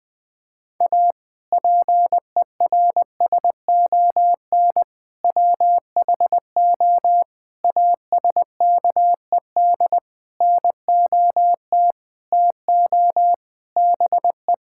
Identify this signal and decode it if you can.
Morse code, heard as a keyed tone, about 20 wpm, 700 Hz. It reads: A PERSON WHO ASKED NOT TO BE
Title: Sentences with 6-Word Phrases - 20wpm